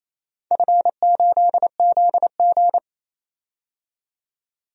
Morse code, heard as a keyed tone, about 28 wpm, 700 Hz. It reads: F8ZG
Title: DXCC-100 Call Signs - 4 Characters - 28wpm